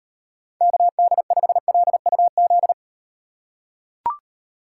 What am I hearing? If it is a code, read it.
Morse code, heard as a keyed tone, about 36 wpm, 700 Hz. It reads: KD5LUZ